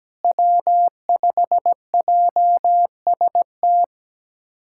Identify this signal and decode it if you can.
Morse code, heard as a keyed tone, about 17 wpm, 700 Hz. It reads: W5JST
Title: USA Callsign DE Callsign - 5 Characters - 17wpm